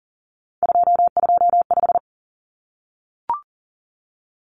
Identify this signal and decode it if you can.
Morse code, heard as a keyed tone, about 40 wpm, 700 Hz. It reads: 225
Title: Common Cut Numbers - 3-Digit 40wpm